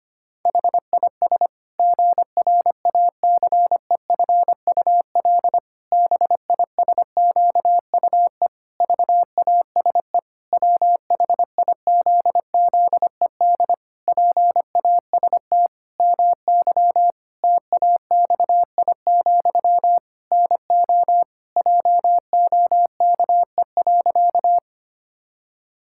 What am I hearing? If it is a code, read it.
Morse code, heard as a keyed tone, about 25 wpm, 700 Hz. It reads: HIS GRACEFUL BISQUE VASE WHIZZED PAST MY TAXI, NO JOKE.
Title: Pangram Challenge v2 - 25wpm